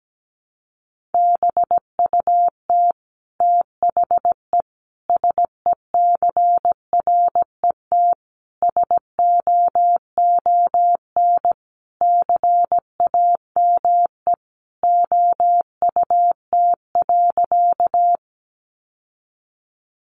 Morse code, heard as a keyed tone, about 17 wpm, 700 Hz. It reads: BUT THE SECRET SOON CAME OUT.
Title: Sentences from Top 1200 Words 17wpm